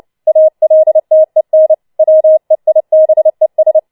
Morse code, heard as a keyed tone, about 29 wpm, 630 Hz. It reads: ALTEN WEIBES
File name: Schneewittchen Teil3 30wpm